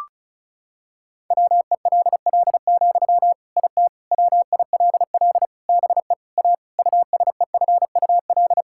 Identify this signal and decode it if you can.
Morse code, heard as a keyed tone, about 34 wpm, 700 Hz. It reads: WELL, IT WILL BE A USEFUL
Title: Sentences from Top 1000 Words - Review Part 6 - 35wpm